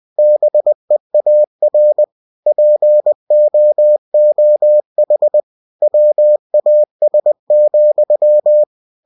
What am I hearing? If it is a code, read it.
Morse code, heard as a keyed tone, about 20 wpm, 600 Hz. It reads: BEAR POOH WAS,
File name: cw-books / winnie-the-pooh / winnie-20wpm-page-0099